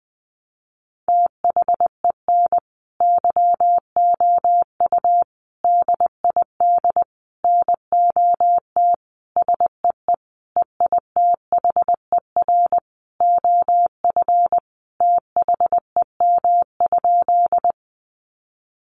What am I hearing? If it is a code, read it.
Morse code, heard as a keyed tone, about 20 wpm, 700 Hz. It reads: THEN YOU DID NOT SEE EITHER OF THEM?
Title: Sentences from Top 600 Words 20wpm